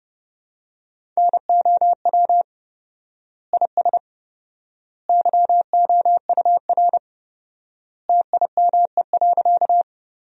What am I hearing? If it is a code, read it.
Morse code, heard as a keyed tone, about 30 wpm, 700 Hz. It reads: NOW IS YOUR TIME.
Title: 4x Word-Spacing - Sentences from Top 100 Words - 30wpm